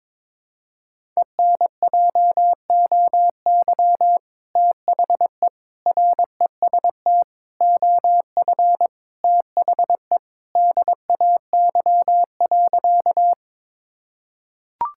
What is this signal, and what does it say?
Morse code, heard as a keyed tone, about 22 wpm, 700 Hz. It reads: ENJOY THE REST OF THE DAY.
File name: practice / 22wpm / Sentences with 2-Word Phrases - 22wpm